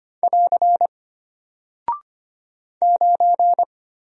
Morse code, heard as a keyed tone, about 25 wpm, 700 Hz. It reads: <AR> 9
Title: Single Character (Letter:Number:Punctuation:Prosign) Alt 25wpm